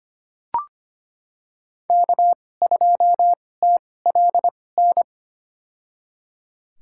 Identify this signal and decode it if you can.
Morse code, heard as a keyed tone, about 25 wpm, spaced as for 18 wpm, 700 Hz. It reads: K2TLN
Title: USA Call Signs - 2x Character Spacing - 25wpm